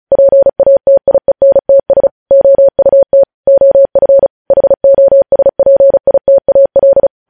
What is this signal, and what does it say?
Morse code, heard as a keyed tone, about 36 wpm, 550 Hz. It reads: PATIENTS OUT OF HOSPITAL
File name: podcast / News headlines-1x-35x35